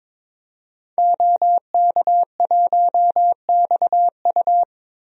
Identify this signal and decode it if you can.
Morse code, heard as a keyed tone, about 22 wpm, 700 Hz. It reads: OK1XU
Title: DXCC-100 Call Signs - 5 Characters - Repeated 2x - 22wpm